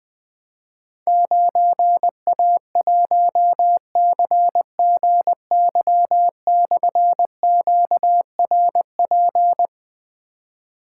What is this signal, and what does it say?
Morse code, heard as a keyed tone, about 20 wpm, 700 Hz. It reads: 9A1CGY/QRP
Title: Difficult Call Signs 20wpm